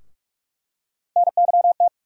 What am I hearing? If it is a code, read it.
Morse code, heard as a keyed tone, about 46 wpm, 700 Hz. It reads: NYT
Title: USA Call Sign Postfixes - 3 Letters 45wpm